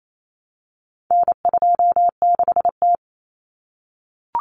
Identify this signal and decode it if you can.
Morse code, heard as a keyed tone, about 28 wpm, 700 Hz. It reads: N26T
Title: Common Cut Numbers - 4-Digit 28wpm